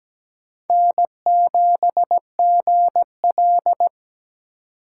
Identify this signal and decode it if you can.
Morse code, heard as a keyed tone, about 20 wpm, 700 Hz. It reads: N7GL
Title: Speed Racing - USA 1x2 Call Signs 17wpm